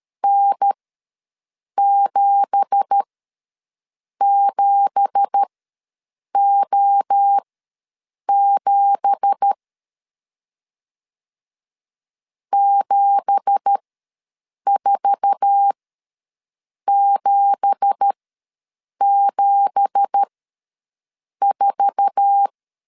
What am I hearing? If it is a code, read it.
Morse code, heard as a keyed tone, about 13 wpm, spaced as for 6 wpm, 800 Hz. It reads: N77O7 74774
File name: LEZIONI / 9 2867 ws 12wpm 3431